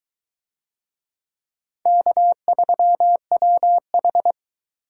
Morse code, heard as a keyed tone, about 22 wpm, 700 Hz. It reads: K3WH